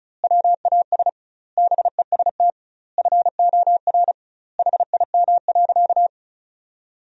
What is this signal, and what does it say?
Morse code, heard as a keyed tone, about 36 wpm, 700 Hz. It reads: WAS BEST FOR HIM.
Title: 2x Word-Spacing - Sentences from Top 500 Words - 35wpm